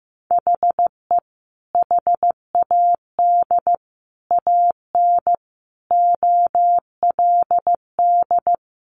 Morse code, heard as a keyed tone, about 15 wpm, 700 Hz. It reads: HE HAD AN OLD
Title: Sentences from Top 1100 Words 15wpm